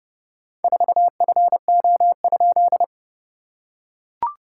Morse code, heard as a keyed tone, about 30 wpm, 700 Hz. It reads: 4FO?
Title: SOTA QSO Practice 30wpm